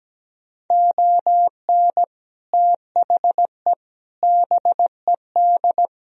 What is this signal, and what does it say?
Morse code, heard as a keyed tone, about 17 wpm, 700 Hz. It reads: ON THE BED
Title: Sentences from Top 1000 Words - Review Part 2 - 17wpm